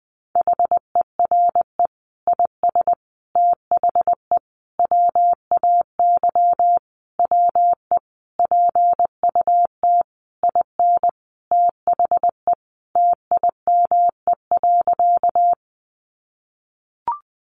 Morse code, heard as a keyed tone, about 19 wpm, 700 Hz. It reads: HERE IS THE WAY WE PUT IN THE TIME.
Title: Sentences from Top 200 Words 20wpm